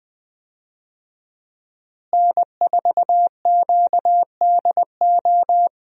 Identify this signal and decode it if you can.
Morse code, heard as a keyed tone, about 20 wpm, 700 Hz. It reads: N4QDO
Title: USA Call Signs - 5 Characters - Repeated 2x - 20wpm